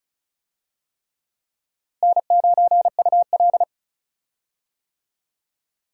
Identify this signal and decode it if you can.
Morse code, heard as a keyed tone, about 36 wpm, 700 Hz. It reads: N9UL
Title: USA Call Signs - 4 Characters - Repeated 3x - 35wpm